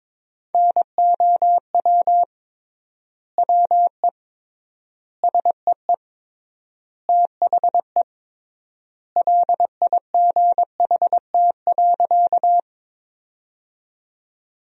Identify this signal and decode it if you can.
Morse code, heard as a keyed tone, about 22 wpm, 700 Hz. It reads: NOW WE SEE THE LIGHT.
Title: 3x Word-Spacing - Sentences from Top 200 Words - 22wpm